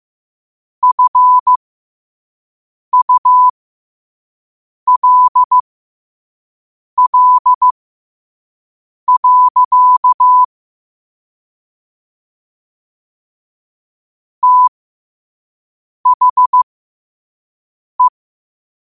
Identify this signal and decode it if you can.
Morse code, heard as a keyed tone, about 15 wpm, spaced as for 5 wpm, 1000 Hz. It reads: FULL. THE
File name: 5wpm